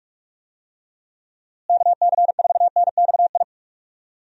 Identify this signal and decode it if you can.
Morse code, heard as a keyed tone, about 46 wpm, 700 Hz. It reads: KC4NXI
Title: USA Call Signs - 6 Characters - Repeated 2x - 45wpm